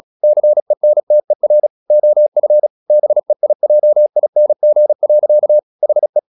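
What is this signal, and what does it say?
Morse code, heard as a keyed tone, about 36 wpm, 600 Hz. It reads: CENTER OF BEIJING. HE
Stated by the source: Morse practice at 35 wpm from CW med Gnister, sending Ken Follett's novel Never in English.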